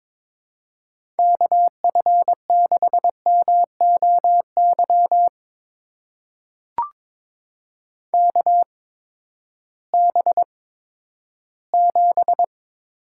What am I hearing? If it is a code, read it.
Morse code, heard as a keyed tone, about 22 wpm, spaced as for 13 wpm, 700 Hz. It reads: KF6MOY KB7
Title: USA Call Signs - 6 Characters - 8x Character Spacing - 22wpm